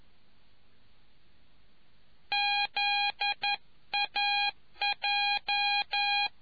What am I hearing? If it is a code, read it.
Morse code, heard as a keyed tone, about 11 wpm, 820 Hz. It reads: ZAJ